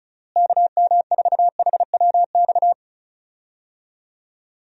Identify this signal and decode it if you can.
Morse code, heard as a keyed tone, about 34 wpm, 700 Hz. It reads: KM4HWX
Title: USA Call Signs - 6 Characters - Repeated 2x - 35wpm